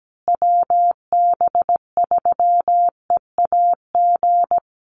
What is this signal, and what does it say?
Morse code, heard as a keyed tone, about 17 wpm, 700 Hz. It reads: WB3EAG
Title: USA Call Signs Repeated 3x 17wpm